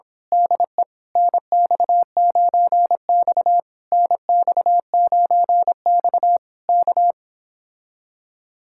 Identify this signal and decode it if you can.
Morse code, heard as a keyed tone, about 26 wpm, 700 Hz. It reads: DE NX9X NX9X K
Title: Speed-Racing Common QSO Phrases - 22wpm